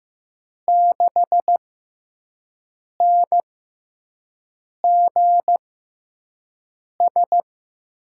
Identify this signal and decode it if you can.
Morse code, heard as a keyed tone, about 15 wpm, spaced as for 5 wpm, 700 Hz. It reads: 6NGS
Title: USA Call Signs - 5 Characters - 6x Character Spacing - 15wpm